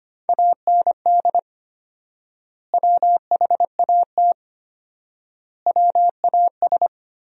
Morse code, heard as a keyed tone, about 25 wpm, 700 Hz. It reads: AND WHAT WAS
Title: 4x Word-Spacing - Sentences from Top 200 Words - 25wpm